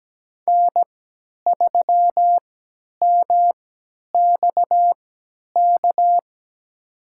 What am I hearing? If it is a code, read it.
Morse code, heard as a keyed tone, about 17 wpm, spaced as for 10 wpm, 700 Hz. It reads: N3MXK
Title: USA Call Signs - 5 Characters - 3x Character Spacing - 17wpm